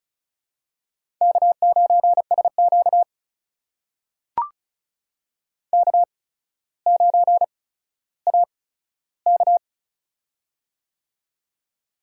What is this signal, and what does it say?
Morse code, heard as a keyed tone, about 34 wpm, spaced as for 15 wpm, 700 Hz. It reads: K9SQ K9AK
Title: USA Call Signs - 4 Characters - 8x Character Spacing - 35wpm